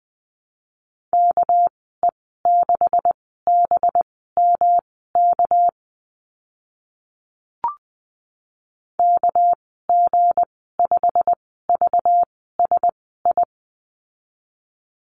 Morse code, heard as a keyed tone, about 20 wpm, spaced as for 14 wpm, 700 Hz. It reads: KE6BMK KG5VSI
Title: 219 Lesson 35g - Number 0 - Call Signs 20wpm